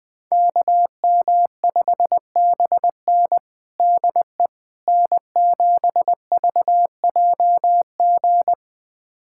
Morse code, heard as a keyed tone, about 20 wpm, 700 Hz. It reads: KM5BN DE N7VJG
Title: USA Callsign DE Callsign - 5 Characters - 20wpm